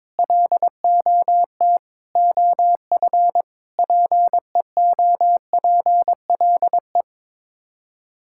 Morse code, heard as a keyed tone, about 22 wpm, 700 Hz. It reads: LOT OF PEOPLE
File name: practice / 22wpm / Build A Phrase - 5-Word Phrases - 22wpm